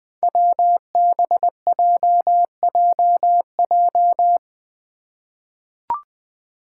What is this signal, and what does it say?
Morse code, heard as a keyed tone, about 20 wpm, 700 Hz. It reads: WBJJJ
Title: Five Challenging Letter-Number Combinations 20wpm